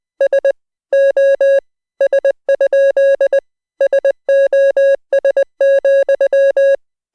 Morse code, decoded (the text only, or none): S O S? SOS,